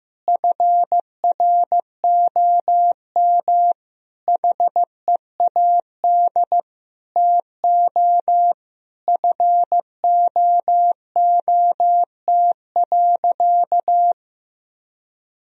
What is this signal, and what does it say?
Morse code, heard as a keyed tone, about 15 wpm, 700 Hz. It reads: FROM HEAD TO FOOT.
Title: Sentences from Top 1000 Words - Review Part 3 - 15wpm